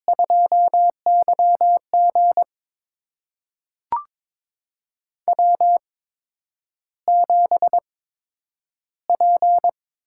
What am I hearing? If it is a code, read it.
Morse code, heard as a keyed tone, about 22 wpm, spaced as for 9 wpm, 700 Hz. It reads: 2YG W7P
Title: USA Call Signs - 4 Characters - 8x Character Spacing - 22wpm